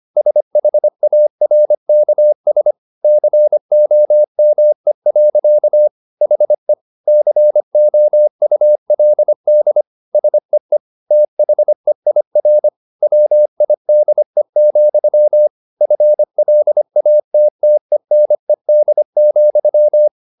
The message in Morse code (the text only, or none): SHARKS COME. HE COULD SEE THEIR WIDE, FLATTENED,